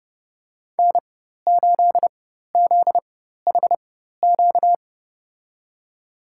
Morse code, heard as a keyed tone, about 29 wpm, spaced as for 14 wpm, 700 Hz. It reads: N8ZHQ
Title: USA Call Signs - 5 Characters - 4x Character Spacing - 30wpm